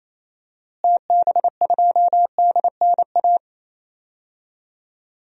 Morse code, heard as a keyed tone, about 28 wpm, 700 Hz. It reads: TB2DNA
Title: DXCC-100 Call Signs - 6 Characters - Repeated 2x - 28wpm